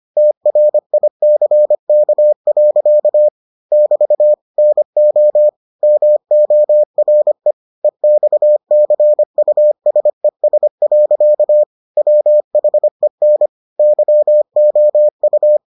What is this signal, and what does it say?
Morse code, heard as a keyed tone, about 25 wpm, 600 Hz. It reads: TRICK. <BT> NO MORE EXCUSES. WHEN YOU